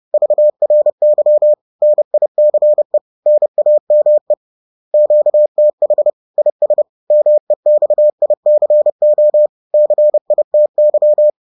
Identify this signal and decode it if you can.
Morse code, heard as a keyed tone, about 30 wpm, 600 Hz. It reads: VRY NICE NAME QTH IS MEXICO CITY